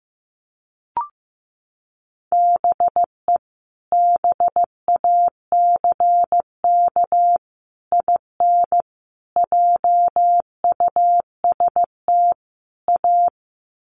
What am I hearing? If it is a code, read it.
Morse code, heard as a keyed tone, about 15 wpm, 700 Hz. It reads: BE BACK IN JUST A
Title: Sets of 5 Words Encore 15wpm